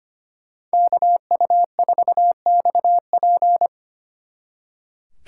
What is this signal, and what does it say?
Morse code, heard as a keyed tone, about 30 wpm, 700 Hz. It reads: KU4XP
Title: Speed Racing - USA Call Signs 25wpm